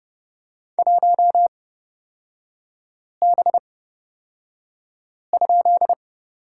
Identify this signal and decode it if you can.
Morse code, heard as a keyed tone, about 30 wpm, spaced as for 5 wpm, 700 Hz. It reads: 1B?
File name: Lesson 37e - Letter x - Three Characters 30wpm